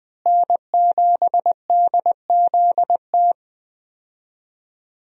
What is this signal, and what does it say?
Morse code, heard as a keyed tone, about 20 wpm, 700 Hz. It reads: N7DZT
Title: USA Call Signs - 5 Characters - Repeated 2x - 20wpm